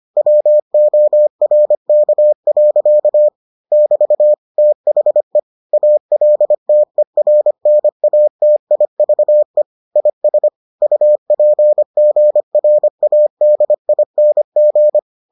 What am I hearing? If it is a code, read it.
Morse code, heard as a keyed tone, about 25 wpm, 600 Hz. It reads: WORK. <BT> THE ALTERNATIVE IS UPGRADING